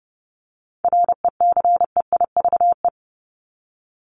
Morse code, heard as a keyed tone, about 30 wpm, 700 Hz. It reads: RECEIVE
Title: Top 751-1000 Words 30wpm